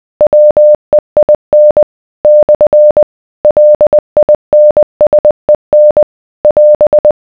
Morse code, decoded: WEIN / LINSEN <AS>